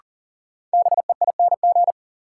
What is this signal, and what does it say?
Morse code, heard as a keyed tone, about 40 wpm, 700 Hz. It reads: BEING
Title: Sets of 4 Words - New Word Drill - 40wpm